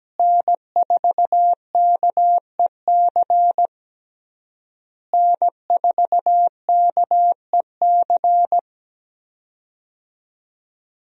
Morse code, heard as a keyed tone, about 17 wpm, 700 Hz. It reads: N4KEC N4KEC
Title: USA Call Signs - 5 Characters - Repeated 3x - 17wpm